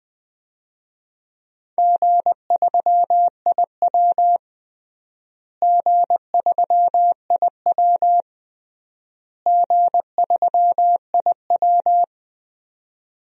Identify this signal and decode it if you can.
Morse code, heard as a keyed tone, about 20 wpm, 700 Hz. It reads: G3IW G3IW G3IW
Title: DXCC-100 Call Signs - 4 Characters - Repeated 3x - 20wpm